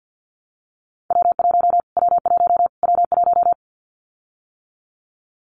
Morse code, heard as a keyed tone, about 50 wpm, 700 Hz. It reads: W1 W1 W1
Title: Callsign Prefixes for Top 25 DXCC - Repeated 3x - 50wpm